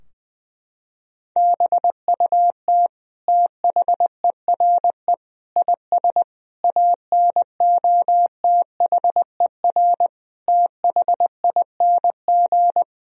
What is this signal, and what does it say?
Morse code, heard as a keyed tone, about 20 wpm, 700 Hz. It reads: BUT THERE IS ANOTHER THING
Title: Sentences from Top 1000 Words - Review Part 4 - 20wpm